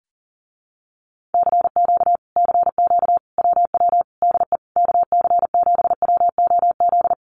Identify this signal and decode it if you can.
Morse code, heard as a keyed tone, about 40 wpm, 700 Hz. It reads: CQ CQ WW DE KC7WOZ
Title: Common QSO Phrases Encore 40wpm